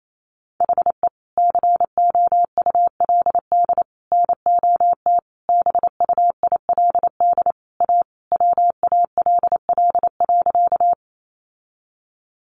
Morse code, heard as a keyed tone, about 28 wpm, 700 Hz. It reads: HE COULD NOT BUILD A WALL.
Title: Sentences from Top 1100 Words 28wpm